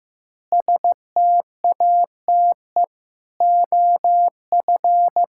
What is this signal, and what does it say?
Morse code, heard as a keyed tone, about 15 wpm, 700 Hz. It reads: STATE OF
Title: Pangram Challenge v2 - 15wpm